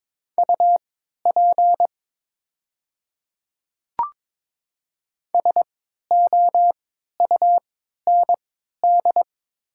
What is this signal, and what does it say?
Morse code, heard as a keyed tone, about 22 wpm, spaced as for 12 wpm, 700 Hz. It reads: UP SOUND
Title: 3x Character Spacing - Top 100 Words - 22wpm